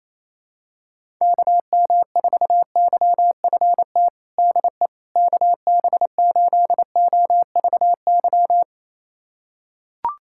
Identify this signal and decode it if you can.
Morse code, heard as a keyed tone, about 28 wpm, 700 Hz. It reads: KM4YFT DE KB8OVY
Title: USA Callsign DE Callsign - 6 Characters - 28wpm